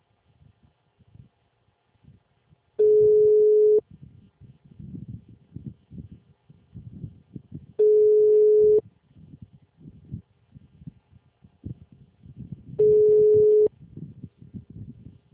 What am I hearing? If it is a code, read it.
Ringback tone (Europe, 425 Hz), with speech.